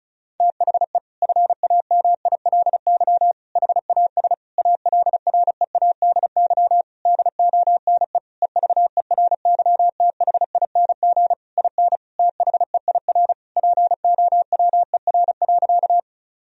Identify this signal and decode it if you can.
Morse code, heard as a keyed tone, about 34 wpm, 700 Hz. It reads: THE FAMILY HAS ALREADY DONE EVERYTHING IN THEIR POWER.
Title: Sentences from Top 1000 Words - Review Part 2 - 35wpm